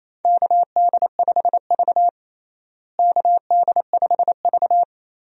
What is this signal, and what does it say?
Morse code, heard as a keyed tone, about 27 wpm, 700 Hz. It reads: KD5V KD5V
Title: USA Call Signs - 4 Characters - Repeated 3x - 28wpm